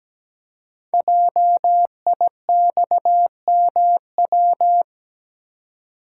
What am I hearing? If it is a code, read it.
Morse code, heard as a keyed tone, about 17 wpm, 700 Hz. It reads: JIXMW
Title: Code Groups - 5 Letters - 17wpm